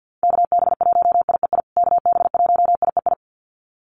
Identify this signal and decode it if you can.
Morse code, heard as a keyed tone, about 50 wpm, 700 Hz. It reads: KB1IEI KB1IEI
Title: USA Call Signs Repeated 2x 50wpm